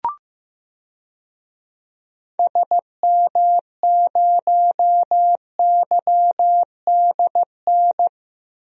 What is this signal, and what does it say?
Morse code, heard as a keyed tone, about 16 wpm, 700 Hz. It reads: SM0YDN
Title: DXCC-100 Call Signs - 6 Characters - Repeated 2x - 15wpm